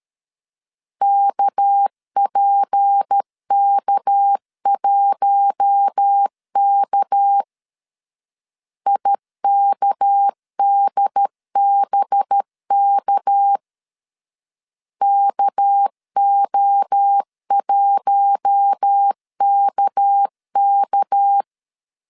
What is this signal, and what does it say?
Morse code, heard as a keyed tone, about 13 wpm, 800 Hz. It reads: KPK1K IKDBK KO1KK
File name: LEZIONI / 6 PKX0 / 6 PK 12wpm 2006